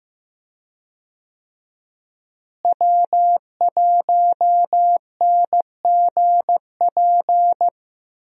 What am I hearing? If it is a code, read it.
Morse code, heard as a keyed tone, about 15 wpm, 700 Hz. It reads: W1NGP
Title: USA Call Signs - 5 Characters - Repeated 3x - 15wpm